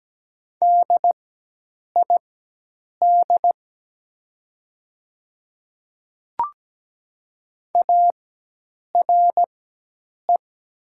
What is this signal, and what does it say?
Morse code, heard as a keyed tone, about 17 wpm, spaced as for 8 wpm, 700 Hz. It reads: DID ARE
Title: 4x Character Spacing - Top 200 Words - 17wpm